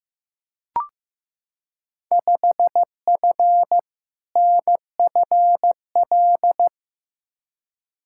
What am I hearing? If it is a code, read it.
Morse code, heard as a keyed tone, about 15 wpm, 700 Hz. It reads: 5F NFL
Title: Field Day Exchanges Encore - 15wpm